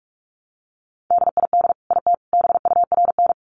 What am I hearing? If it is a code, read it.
Morse code, heard as a keyed tone, about 44 wpm, 700 Hz. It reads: DID IT BURN
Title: Sentences from Top 1300 Words 45wpm